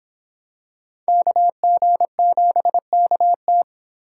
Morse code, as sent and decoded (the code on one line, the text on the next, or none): -.- --. --... -.- -
KG7KT